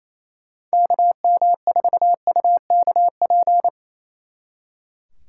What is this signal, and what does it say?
Morse code, heard as a keyed tone, about 35 wpm, 700 Hz. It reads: KM4UKP
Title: Speed Racing - USA 2x3 Call Signs 28wpm